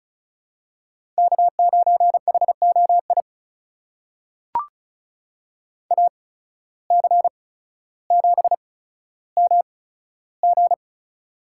Morse code, heard as a keyed tone, about 36 wpm, spaced as for 15 wpm, 700 Hz. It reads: K9HOI AC7MG
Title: USA Call Signs - 5 Characters - 8x Character Spacing - 35wpm